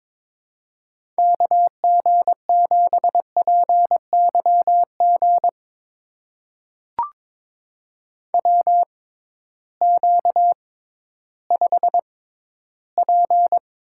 Morse code, heard as a keyed tone, about 22 wpm, spaced as for 13 wpm, 700 Hz. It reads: KG7PYG WQ5P
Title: USA Call Signs - 6x Character Spacing - 22wpm